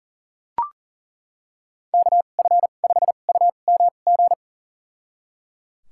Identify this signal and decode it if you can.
Morse code, heard as a keyed tone, about 38 wpm, spaced as for 28 wpm, 690 Hz. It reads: KF5UMG
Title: USA Call Signs - 6 Characters - 2x Character Spacing - 40wpm